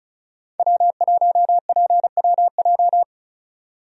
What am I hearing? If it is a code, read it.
Morse code, heard as a keyed tone, about 34 wpm, 700 Hz. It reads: W1PWJ
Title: USA Call Signs - 5 Characters - 35wpm